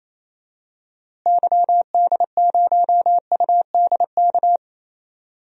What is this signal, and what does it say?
Morse code, heard as a keyed tone, about 28 wpm, 700 Hz. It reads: YD0UDK
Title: DXCC-100 Call Signs - 6 Characters - Repeated 3x - 28wpm